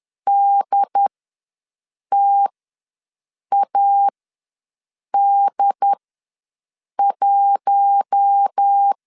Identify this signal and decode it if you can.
Morse code, heard as a keyed tone, about 10 wpm, spaced as for 6 wpm, 800 Hz. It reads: DTAD1
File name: LEZIONI / 5 LBVC / 5 LBVC ws 10wpm 3733